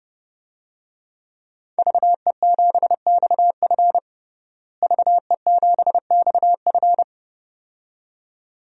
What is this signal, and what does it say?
Morse code, heard as a keyed tone, about 30 wpm, 700 Hz. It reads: VE7XF VE7XF
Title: DXCC-100 Call Signs - 5 Characters - Repeated 2x - 30wpm